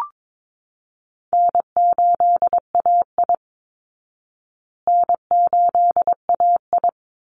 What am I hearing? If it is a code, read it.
Morse code, heard as a keyed tone, about 22 wpm, 700 Hz. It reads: N8AI N8AI
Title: Easy Call Signs Repeated 3x - 22wpm